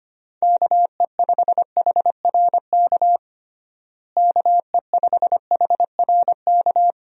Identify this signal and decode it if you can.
Morse code, heard as a keyed tone, about 25 wpm, 700 Hz. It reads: KE5HRK KE5HRK
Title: USA Call Signs - 6 Characters - Repeated 2x - 25wpm